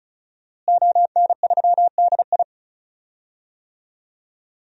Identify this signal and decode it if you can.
Morse code, heard as a keyed tone, about 36 wpm, 700 Hz. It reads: ON3DI
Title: DXCC-100 Call Signs - 5 Characters - Repeated 3x - 35wpm